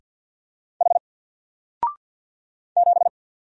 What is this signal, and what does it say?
Morse code, heard as a keyed tone, about 50 wpm, 700 Hz. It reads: H 7
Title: Single Letter-Number 50wpm